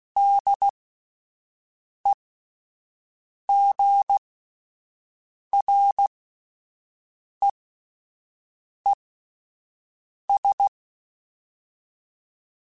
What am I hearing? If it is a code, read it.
Morse code, heard as a keyed tone, about 16 wpm, spaced as for 5 wpm, 800 Hz. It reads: DEGREES
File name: Spaced Characters / Spaced Characters-mp3s / Weather4 21 sp @10